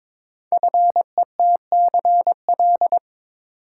Morse code, heard as a keyed tone, about 22 wpm, 700 Hz. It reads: FETCL
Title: Code Groups - 5 Letters - 22wpm